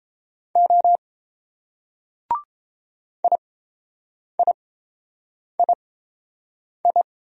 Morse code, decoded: O IIII

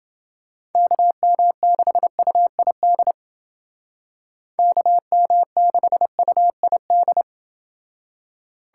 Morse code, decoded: KM6UID KM6UID

